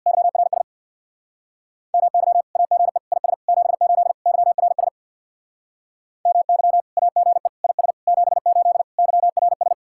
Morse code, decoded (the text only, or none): YRS MY AGE IS 78 YRS MY AGE IS 78 YRS